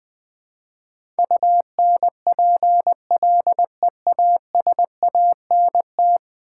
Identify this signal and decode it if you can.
Morse code, heard as a keyed tone, about 20 wpm, 700 Hz. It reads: UNPLEASANT
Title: Build A Word - 5 Letters At A Time - 10-Letter Words - 20wpm